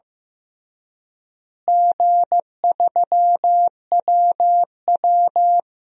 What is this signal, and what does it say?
Morse code, heard as a keyed tone, about 15 wpm, 700 Hz. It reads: G3WW
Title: DXCC-100 Call Signs - 4 Characters - Repeated 3x - 15wpm